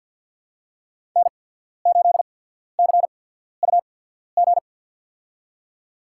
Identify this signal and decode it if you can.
Morse code, heard as a keyed tone, about 53 wpm, spaced as for 14 wpm, 700 Hz. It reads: N8CUG
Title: USA Call Signs - 5 Characters - 8x Character Spacing - 50wpm